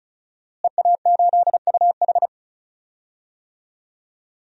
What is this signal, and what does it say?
Morse code, heard as a keyed tone, about 34 wpm, 700 Hz. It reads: EA8UH
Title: DXCC-100 Call Signs - 5 Characters - Repeated 2x - 35wpm